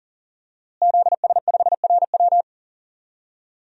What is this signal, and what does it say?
Morse code, heard as a keyed tone, about 40 wpm, 700 Hz. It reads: ZS5RW
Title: DXCC-100 Call Signs - 5 Characters - Repeated 3x - 40wpm